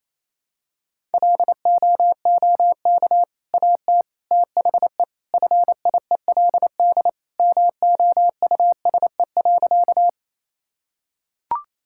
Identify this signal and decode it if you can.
Morse code, heard as a keyed tone, about 28 wpm, 700 Hz. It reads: LOOK AT THE FIELD MOUSE.
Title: Sentences from Top 1000 Words - Review Part 5 - 28wpm